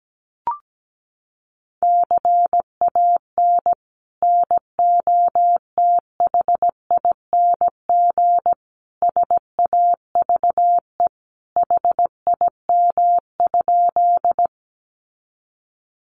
Morse code, heard as a keyed tone, about 17 wpm, 700 Hz. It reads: CAN NOTHING SAVE HIM?
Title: Sentences from Top 1100 Words 17wpm